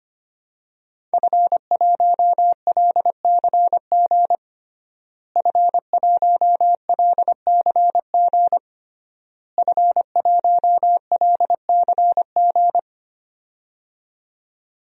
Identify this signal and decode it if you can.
Morse code, heard as a keyed tone, about 25 wpm, 700 Hz. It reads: F1LCG F1LCG F1LCG
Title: DXCC-100 Call Signs - 5 Characters - Repeated 3x - 25wpm